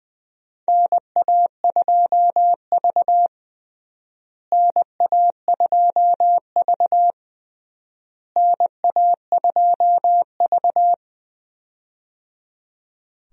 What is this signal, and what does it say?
Morse code, heard as a keyed tone, about 20 wpm, 700 Hz. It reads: NA2V NA2V NA2V
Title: USA Call Signs - 4 Characters - Repeated 3x - 20wpm